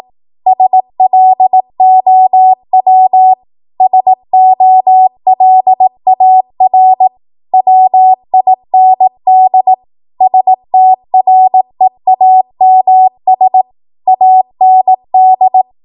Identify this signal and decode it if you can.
Morse code, heard as a keyed tone, about 18 wpm, 750 Hz. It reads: SLOW SOLAR WIND STREAMS AND
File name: Morse / 18WPM